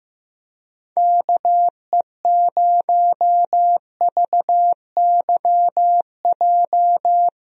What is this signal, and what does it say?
Morse code, heard as a keyed tone, about 16 wpm, 700 Hz. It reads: KE0VYJ
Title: USA Call Signs - 6 Characters - 15wpm